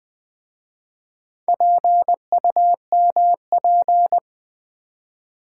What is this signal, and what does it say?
Morse code, heard as a keyed tone, about 20 wpm, 700 Hz. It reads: PUMP
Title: Easy Words with Hints - 20wpm